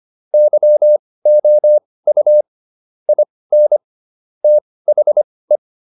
Morse code, heard as a keyed tone, about 25 wpm, spaced as for 18 wpm, 600 Hz. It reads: YOU IN THE